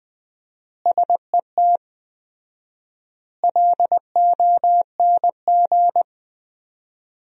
Morse code, heard as a keyed tone, about 20 wpm, 700 Hz. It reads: SET LONG
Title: 4x Word-Spacing - Sentences from Top 300 Words - 20wpm